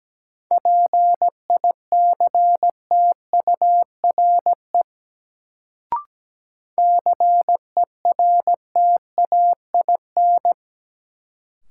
Morse code, heard as a keyed tone, about 17 wpm, 700 Hz. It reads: PICTURE CERTAIN
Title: Sentences from Top 500 Words - New Word Drill - 17wpm